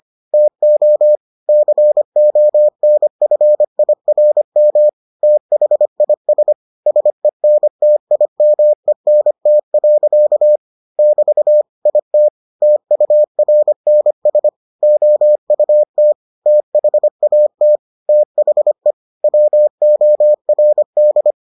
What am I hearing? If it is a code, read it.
Morse code, heard as a keyed tone, about 25 wpm, 600 Hz. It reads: TO CONFIRM THIS SENTIMENT. <BT> IT TURNS OUT THAT THE WORD